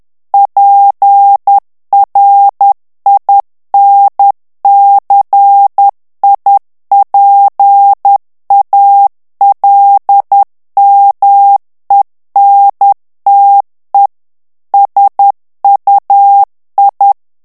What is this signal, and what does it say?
Morse code, heard as a keyed tone, about 10 wpm, 800 Hz. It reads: PRINCIPALMENTE SUI
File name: morse mp3 / ITALIEN texte clair 1 vitesse 10 mots minute